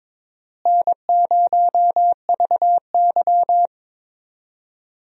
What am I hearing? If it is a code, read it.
Morse code, heard as a keyed tone, about 22 wpm, 700 Hz. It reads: N0VY
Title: USA Call Signs - 4 Characters - Repeated 2x - 22wpm